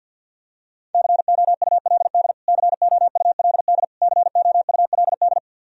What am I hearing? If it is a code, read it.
Morse code, heard as a keyed tone, about 52 wpm, 700 Hz. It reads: COULD COULD COULD
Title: Top 100 Words Repeated 50wpm